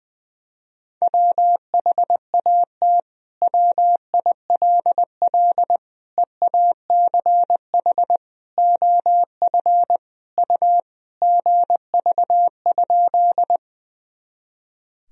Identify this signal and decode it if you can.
Morse code, heard as a keyed tone, about 20 wpm, 700 Hz. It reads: WHAT WILL EACH OF U GV?
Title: Sentences from Top 200 Words and Abbreviations 20wpm